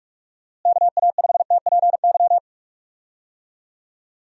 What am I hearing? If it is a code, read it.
Morse code, heard as a keyed tone, about 46 wpm, 700 Hz. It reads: KA5TPY
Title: USA Call Signs - 6 Characters - Repeated 3x - 45wpm